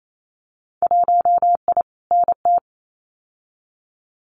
Morse code, heard as a keyed tone, about 28 wpm, 690 Hz. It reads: 1I NT
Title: Winter Field Day Exchanges Encore 28wpm